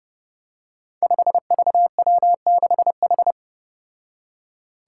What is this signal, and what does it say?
Morse code, heard as a keyed tone, about 32 wpm, 700 Hz. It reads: 5VW6H
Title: Five Challenging Letter-Number Combinations 30wpm